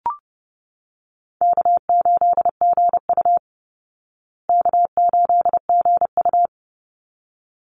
Morse code, heard as a keyed tone, about 30 wpm, 700 Hz. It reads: K8GU K8GU
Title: Easy Call Signs Repeated 2x - 30wpm